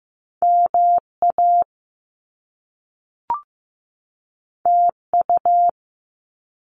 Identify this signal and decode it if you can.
Morse code, heard as a keyed tone, about 15 wpm, 700 Hz. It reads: MA TU
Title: Top Two Letter Combinations in Words 15wpm